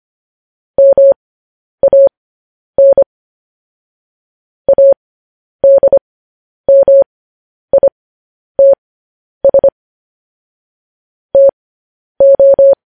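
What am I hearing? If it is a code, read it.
Morse code, heard as a keyed tone, about 25 wpm, spaced as for 10 wpm, 550 Hz. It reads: MAN ADMITS TO
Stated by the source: Morse code news headlines sent at 10.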